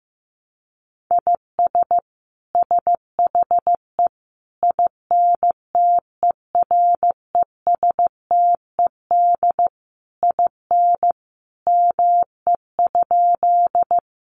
Morse code, heard as a keyed tone, about 15 wpm, 700 Hz. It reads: IS SHE INTERESTED IN ME?
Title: Sentences from Top 900 Words 15wpm